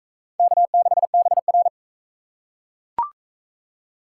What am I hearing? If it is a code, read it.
Morse code, heard as a keyed tone, about 38 wpm, 700 Hz. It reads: K6BR